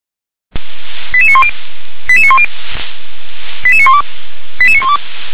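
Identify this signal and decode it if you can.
Five-tone selective call (ZVEI), heard as quick runs of five tones, tones of about 70 ms. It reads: 88910 88910 88912 88912
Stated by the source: ZVEI selective call received on SDR.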